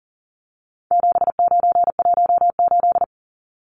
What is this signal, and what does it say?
Morse code, heard as a keyed tone, about 40 wpm, 700 Hz. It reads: 7918
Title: Numbers - 4-Digit 40wpm